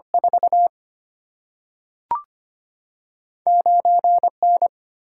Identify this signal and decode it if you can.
Morse code, heard as a keyed tone, about 25 wpm, 700 Hz. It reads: 4 9N
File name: Two Letter-Number Combinations - 25wpm